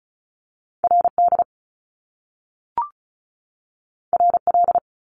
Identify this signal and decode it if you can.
Morse code, heard as a keyed tone, about 36 wpm, 700 Hz. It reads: RD RL
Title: Top Two Letter Combinations in Words 35wpm